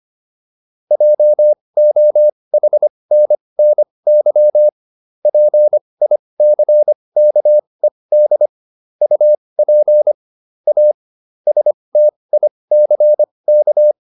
Morse code, heard as a keyed tone, about 25 wpm, spaced as for 20 wpm, 600 Hz. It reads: JOHNNY PICKED UP A STICK